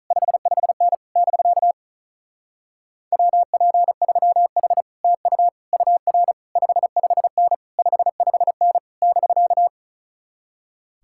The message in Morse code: 55N <BK> WP3H TU UR 55N 55N <BK>

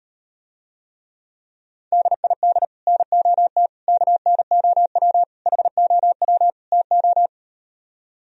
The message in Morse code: DID NOT KNOW HOW TO

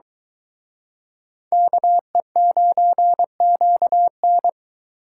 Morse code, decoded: KE9QN

